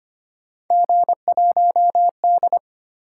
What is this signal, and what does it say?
Morse code, heard as a keyed tone, about 25 wpm, 700 Hz. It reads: G1D
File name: Three Letter-Number Combinations - 25wpm